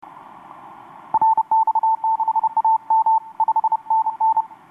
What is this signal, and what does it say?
Morse code, heard as a keyed tone, about 31 wpm, 890 Hz. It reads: RX6AM 5NN
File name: xw2cw rx6am etc 12m cw 26nov2012 1153